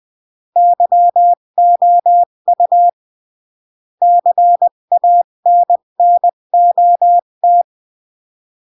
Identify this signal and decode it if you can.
Morse code, heard as a keyed tone, about 20 wpm, 700 Hz. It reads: YOU CANNOT
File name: CW / jodorowsky / 82 commandments 20-15